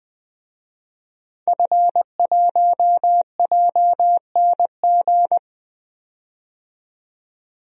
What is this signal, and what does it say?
Morse code, heard as a keyed tone, about 20 wpm, 700 Hz. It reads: F1JNG
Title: DXCC-100 Call Signs - 5 Characters - 20wpm